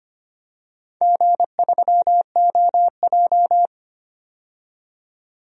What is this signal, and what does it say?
Morse code, heard as a keyed tone, about 25 wpm, 700 Hz. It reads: G3OJ
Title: DXCC-100 Call Signs - 4 Characters - 25wpm